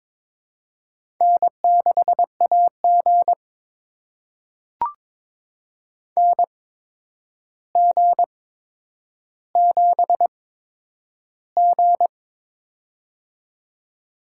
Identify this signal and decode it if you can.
Morse code, heard as a keyed tone, about 22 wpm, spaced as for 9 wpm, 700 Hz. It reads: N6AG NG7G
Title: USA Call Signs - 4 Characters - 8x Character Spacing - 22wpm